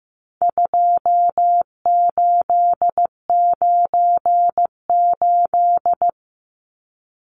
Morse code, heard as a keyed tone, about 15 wpm, 700 Hz. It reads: 2898
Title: Numbers - 4-Digit 15wpm